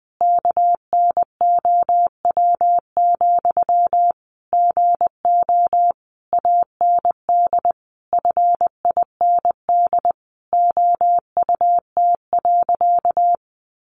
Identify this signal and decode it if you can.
Morse code, heard as a keyed tone, about 20 wpm, 700 Hz. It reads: KNOW, GO AND FIND OUT.
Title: Sentences from Top 200 Words 20wpm